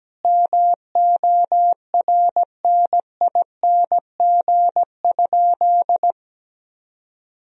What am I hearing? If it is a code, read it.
Morse code, heard as a keyed tone, about 17 wpm, 700 Hz. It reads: MORNING?
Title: Sentences from Top 1000 Words - Review Part 3 - 17wpm